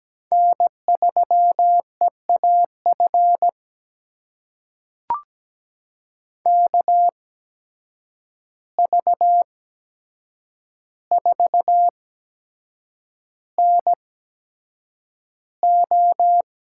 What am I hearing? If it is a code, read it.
Morse code, heard as a keyed tone, about 17 wpm, spaced as for 7 wpm, 700 Hz. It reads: N3EAF KV4NO
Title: USA Call Signs - 5 Characters - 8x Character Spacing - 17wpm